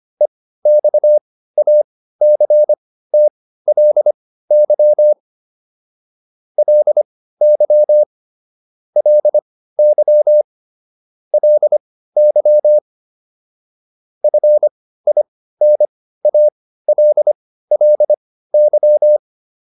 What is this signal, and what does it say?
Morse code, heard as a keyed tone, about 25 wpm, spaced as for 15 wpm, 600 Hz. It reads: EXACTLY LY LY LY FINALLY